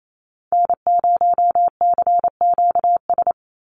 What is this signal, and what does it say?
Morse code, heard as a keyed tone, about 28 wpm, 700 Hz. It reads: N0CQS
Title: USA Call Signs 28wpm